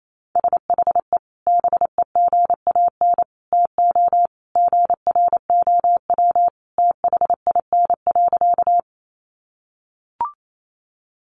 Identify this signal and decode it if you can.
Morse code, heard as a keyed tone, about 28 wpm, 700 Hz. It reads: SHE BEGAN TO GROW THIN.
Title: Sentences from Top 1200 Words 28wpm